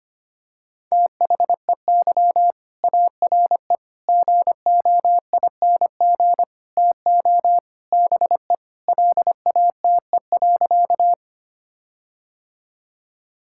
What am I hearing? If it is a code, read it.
Morse code, heard as a keyed tone, about 25 wpm, 700 Hz. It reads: THEY ARE GOING TO BE LATE.
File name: practice / 25wpm / Sentences with 4-Word Phrases - 25wpm